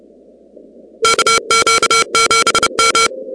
Morse code, heard as a keyed tone, about 30 wpm, 1360 Hz, with noise. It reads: KQ7M
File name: contests / 2019-cw-wpx-kt5j-k5pi- -2330 0-38